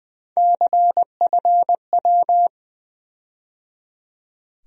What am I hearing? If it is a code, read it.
Morse code, heard as a keyed tone, about 21 wpm, 700 Hz. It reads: CFW
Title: USA Call Sign Postfixes - 3 Letters 20wpm